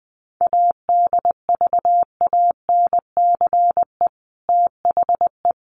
Morse code, heard as a keyed tone, about 20 wpm, 700 Hz. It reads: ADVANCE THE
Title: Sentences from Top 1400 Words 20wpm